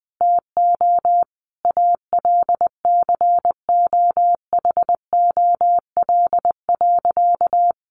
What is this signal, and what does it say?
Morse code, heard as a keyed tone, about 19 wpm, 700 Hz. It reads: TO ALCOHOL.
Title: Sentences from Top 1400 Words 20wpm